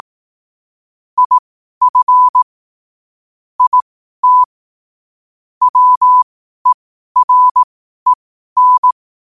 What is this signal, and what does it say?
Morse code, heard as a keyed tone, about 18 wpm, spaced as for 12 wpm, 1000 Hz. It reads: IF IT WEREN
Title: 12 WPM morse code quotes for Thu, 29 Feb 2024